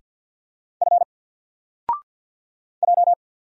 Morse code, heard as a keyed tone, about 50 wpm, 700 Hz. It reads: F J
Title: Single Letter-Number Alt 50wpm